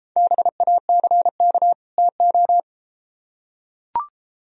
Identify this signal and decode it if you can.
Morse code, heard as a keyed tone, about 33 wpm, 700 Hz. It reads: BACK TO